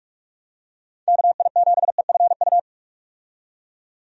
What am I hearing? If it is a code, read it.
Morse code, heard as a keyed tone, about 44 wpm, 700 Hz. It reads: KI7EFU
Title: USA Call Signs - 6 Characters - 45wpm